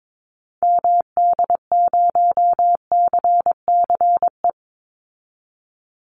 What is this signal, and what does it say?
Morse code, heard as a keyed tone, about 22 wpm, 700 Hz. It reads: MD0CCE
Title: Call Signs Repeated 2x - 22wpm